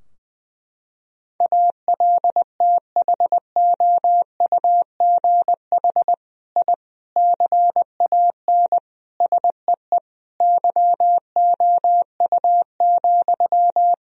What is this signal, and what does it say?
Morse code, heard as a keyed tone, about 20 wpm, 700 Hz. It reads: ALTHOUGH I CAN SEE YOU,
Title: Sentences from Top 1000 Words - Review Part 3 - 20wpm